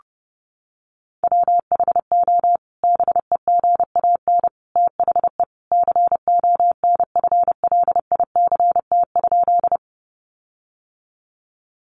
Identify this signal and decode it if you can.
Morse code, heard as a keyed tone, about 30 wpm, 700 Hz. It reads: WHO BEGAN THE CONFLICT?
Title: Sentences from Top 1300 Words 30wpm